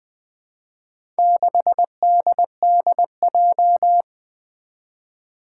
Morse code, heard as a keyed tone, about 20 wpm, 700 Hz. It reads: 6DDJ
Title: Four Challenging Letter-Number Combinations 20wpm